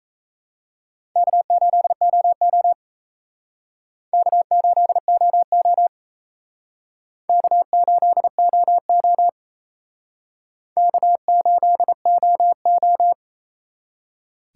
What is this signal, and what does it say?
Morse code, heard as a keyed tone, about 35 wpm, 700 Hz. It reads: K8OO K8OO K8OO K8OO
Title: Speed Racing - USA Call Signs 28wpm